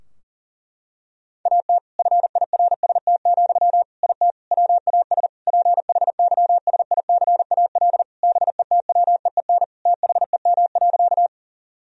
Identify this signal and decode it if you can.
Morse code, heard as a keyed tone, about 40 wpm, 700 Hz. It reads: AT FIRST, IT WAS PHYSICAL BETWEEN THEM.
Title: Sentences from Top 1000 Words - Review Part 4 - 40wpm